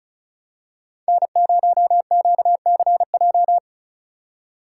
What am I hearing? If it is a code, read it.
Morse code, heard as a keyed tone, about 36 wpm, 700 Hz. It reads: N0QCJ
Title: USA Call Signs - 5 Characters - 3x Character Spacing - 35wpm